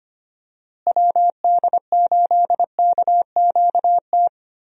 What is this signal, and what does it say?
Morse code, heard as a keyed tone, about 25 wpm, 700 Hz. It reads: WD8KQT
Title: USA Call Signs - 6 Characters - Repeated 3x - 25wpm